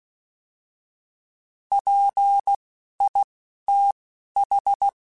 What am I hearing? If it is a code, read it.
Morse code, heard as a keyed tone, about 16 wpm, spaced as for 11 wpm, 800 Hz. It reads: PITH
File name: Alphabetical / Alphabetical mp3s / Ps @10